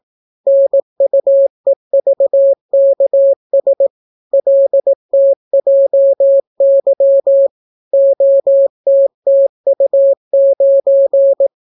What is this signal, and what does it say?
Morse code, heard as a keyed tone, about 18 wpm, 550 Hz. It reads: NUEVKS LTJY OTTU9